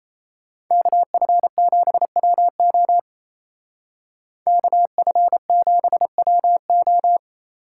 Morse code, heard as a keyed tone, about 36 wpm, 700 Hz. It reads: KF7WO KF7WO